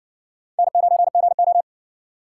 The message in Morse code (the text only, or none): N9GO